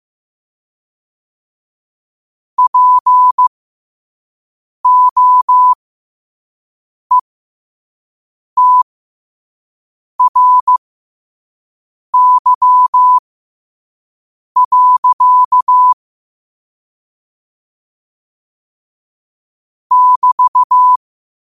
Morse code, heard as a keyed tone, about 15 wpm, spaced as for 5 wpm, 1000 Hz. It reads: POETRY. <BT>
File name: qotdfiles / 5wpm